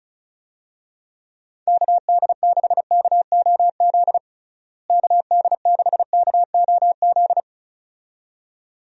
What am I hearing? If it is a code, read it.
Morse code, heard as a keyed tone, about 34 wpm, 700 Hz. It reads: KD6KOZ KD6KOZ
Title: USA Call Signs - 6 Characters - Repeated 2x - 35wpm